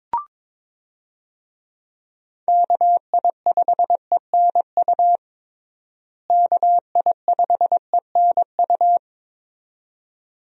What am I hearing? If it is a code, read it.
Morse code, heard as a keyed tone, about 22 wpm, 700 Hz. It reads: KI5ENU KI5ENU
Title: USA Call Signs - 6 Characters - Repeated 2x - 22wpm